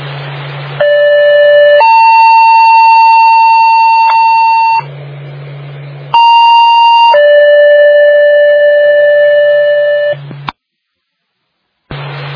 Two-tone page, heard as a one-second tone, then a three-second tone, about 580 then 900 Hz, about 930 then 590 Hz. It is heard from 0.8 to 10.1 s.
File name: Northville 2016 11 21 19 20 25